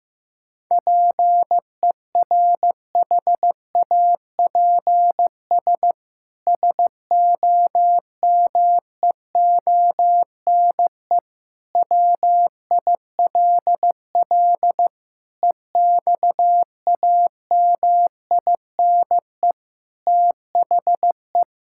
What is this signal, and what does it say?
Morse code, heard as a keyed tone, about 15 wpm, 700 Hz. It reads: PERHAPS SOMEONE WILL EXAMINE THE
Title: Sentences from Top 1600 Words 15wpm